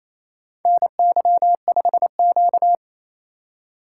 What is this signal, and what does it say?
Morse code, heard as a keyed tone, about 29 wpm, 690 Hz. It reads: NY5Q
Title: USA Call Signs - 4 Characters - 2x Character Spacing - 28wpm